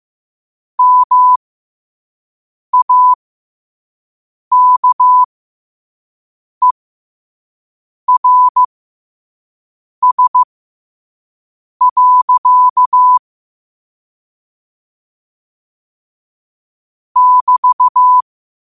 Morse code, heard as a keyed tone, about 15 wpm, spaced as for 5 wpm, 1000 Hz. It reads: MAKERS. <BT>
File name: qotdfiles / 5wpm